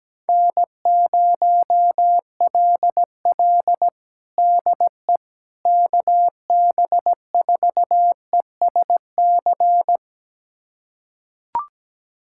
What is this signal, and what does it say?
Morse code, heard as a keyed tone, about 17 wpm, 700 Hz. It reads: N0LL DE KB4ESC
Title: USA Callsign DE Callsign 17wpm